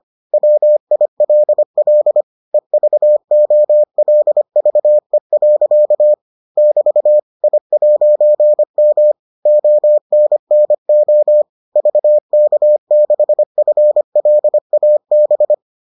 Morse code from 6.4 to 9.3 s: <BT> I'M

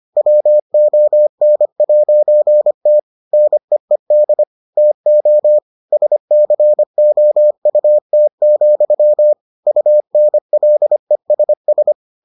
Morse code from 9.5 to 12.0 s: UNLESS

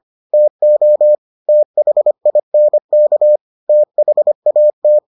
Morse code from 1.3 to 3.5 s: THINK

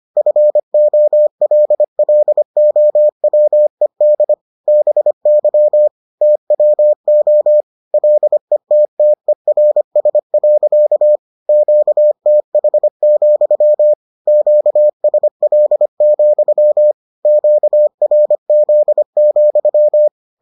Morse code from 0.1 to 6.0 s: FOLLOWED BY

